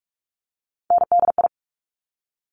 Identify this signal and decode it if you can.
Morse code, heard as a keyed tone, about 46 wpm, 700 Hz. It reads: NDI
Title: Top Three Letter Combinations in Words 45wpm